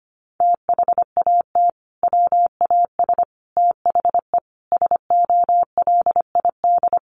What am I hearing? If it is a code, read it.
Morse code, heard as a keyed tone, about 25 wpm, 700 Hz. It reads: THAT WAS THE SOLID
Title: Sentences from Top 1500 Words 25wpm